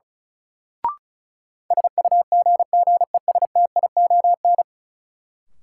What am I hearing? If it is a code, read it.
Morse code, heard as a keyed tone, about 36 wpm, 700 Hz. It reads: SUGGESTION
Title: Sentences from Top 1600 Words - New Word Drill Alt - 35wpm